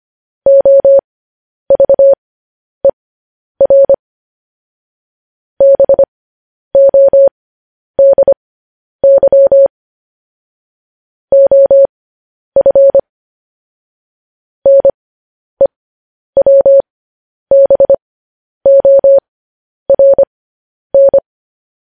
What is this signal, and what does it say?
Morse code, heard as a keyed tone, about 25 wpm, spaced as for 10 wpm, 550 Hz. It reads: OVER BODY OF NEWBORN
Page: news-headlines-in-morse-code-at-10-wpm